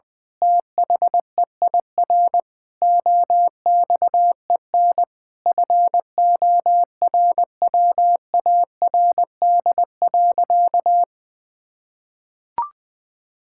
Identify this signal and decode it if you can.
Morse code, heard as a keyed tone, about 20 wpm, 700 Hz. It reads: THEIR OXEN FORWARD.
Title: Pangram Challenge v2 - 20wpm